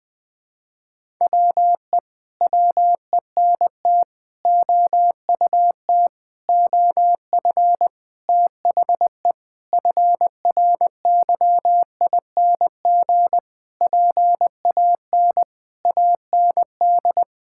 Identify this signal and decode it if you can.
Morse code, heard as a keyed tone, about 20 wpm, 700 Hz. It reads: WE WENT OUT OF THE FRYING PAN AND
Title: Sentences with 9-Word Phrases - 20wpm